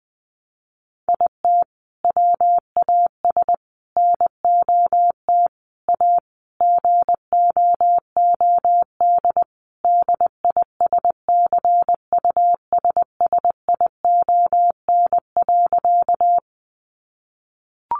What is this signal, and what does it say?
Morse code, heard as a keyed tone, about 20 wpm, 700 Hz. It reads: IT WAS NOT A GOOD DISCUSSION.
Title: Sentences from Top 1100 Words 20wpm